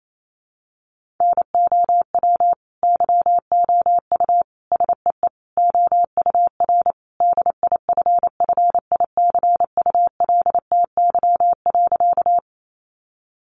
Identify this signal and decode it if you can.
Morse code, heard as a keyed tone, about 28 wpm, 700 Hz. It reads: NOW YOU SEE OUR DIFFICULTY.
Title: Sentences from Top 1500 Words 28wpm